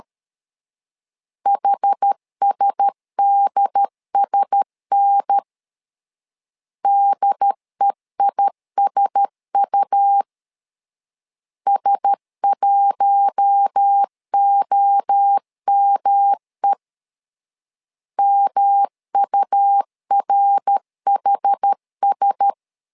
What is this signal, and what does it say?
Morse code, heard as a keyed tone, about 13 wpm, 800 Hz. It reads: HSDSN DEISU S1OME MURHS
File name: LEZIONI / 4 Q1S 12wpm 1802